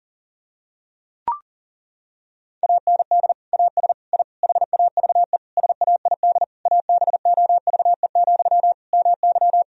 Morse code, heard as a keyed tone, about 37 wpm, 700 Hz. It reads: AND AS I HAVE SAID ABOVE, MY
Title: Sentences from Top 1000 Words - Review Part 2 - 40wpm